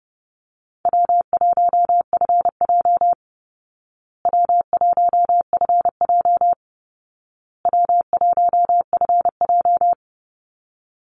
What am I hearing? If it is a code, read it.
Morse code, heard as a keyed tone, about 30 wpm, 700 Hz. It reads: W1FJ W1FJ W1FJ
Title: Easy Call Signs Repeated 3x - 30wpm